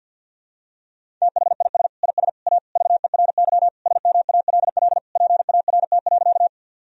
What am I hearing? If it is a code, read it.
Morse code, heard as a keyed tone, about 51 wpm, 700 Hz. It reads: THIS IS A VERY SMALL PART.
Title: Sentences from Top 1000 Words - Review Part 1 - 50wpm